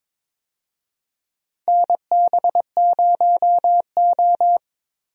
Morse code, heard as a keyed tone, about 22 wpm, 700 Hz. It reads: NB0O
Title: USA Call Signs - 4 Characters - Repeated 2x - 22wpm